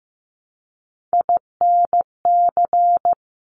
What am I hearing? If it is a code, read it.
Morse code, heard as a keyed tone, about 15 wpm, 690 Hz. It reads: INC
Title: Top Three Letter Combinations in Words 15wpm